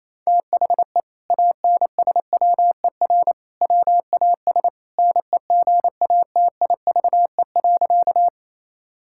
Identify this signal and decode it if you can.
Morse code, heard as a keyed tone, about 28 wpm, 700 Hz. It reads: THE ANSWER WAS NEGATIVE.
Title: Sentences from Top 1000 Words - Review Part 6 - 28wpm